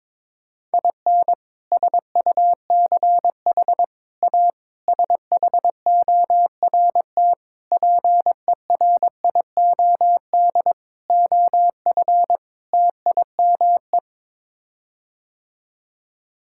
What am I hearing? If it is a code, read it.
Morse code, heard as a keyed tone, about 22 wpm, 700 Hz. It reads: IN SUCH A SHORT PERIOD OF TIME
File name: practice / 22wpm / Sets of 7 Words - 22wpm